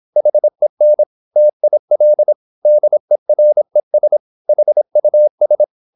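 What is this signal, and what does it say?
Morse code, heard as a keyed tone, about 26 wpm, 600 Hz. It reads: HEN TIL DERES HUS
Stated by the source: High-speed (QRQ) Morse at 26 wpm.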